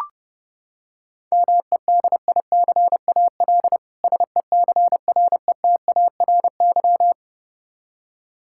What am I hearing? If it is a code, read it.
Morse code, heard as a keyed tone, about 32 wpm, 700 Hz. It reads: MEDICAL SECRETARY
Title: K1USN SST QSO Practice 30wpm